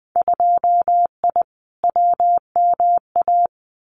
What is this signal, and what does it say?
Morse code, heard as a keyed tone, about 20 wpm, 700 Hz. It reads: 2I WMA
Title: Winter Field Day Exchanges Encore 20wpm